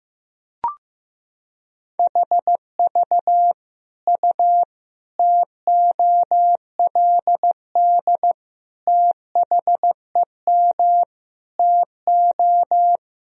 Morse code, heard as a keyed tone, about 15 wpm, 700 Hz. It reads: HV U TOLD THEM TO
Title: Sentences from Top 500 Words and Abbreviations 15wpm